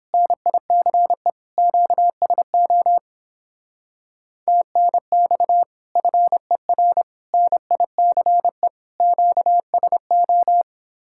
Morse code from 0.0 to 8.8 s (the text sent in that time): NICE QSO TNX FER NICE